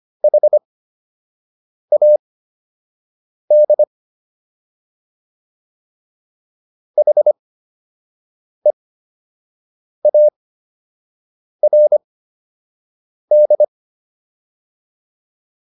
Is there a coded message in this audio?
Yes: Morse code from 0.2 to 13.6 s, heard as a keyed tone, about 25 wpm, spaced as for 6 wpm, 600 Hz. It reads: HAD HEARD